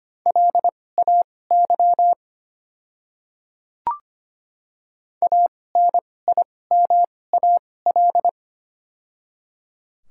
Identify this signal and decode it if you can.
Morse code, heard as a keyed tone, about 25 wpm, spaced as for 18 wpm, 700 Hz. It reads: LAY ANIMAL
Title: 2x Character Spacing - Top 500 Words - 25wpm